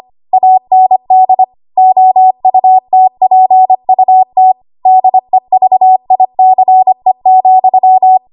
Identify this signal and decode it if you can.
Morse code, heard as a keyed tone, about 25 wpm, 750 Hz. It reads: AND OUTPUT DEVICE,
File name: Archive / 25 WPM / 25WPM